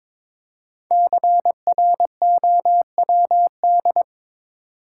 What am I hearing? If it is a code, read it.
Morse code, heard as a keyed tone, about 22 wpm, 700 Hz. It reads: CROWD
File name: 5 Letter Words - First 3 Letters Given - 22wpm